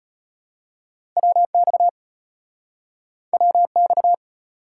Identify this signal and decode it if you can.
Morse code, heard as a keyed tone, about 34 wpm, 700 Hz. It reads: WX WX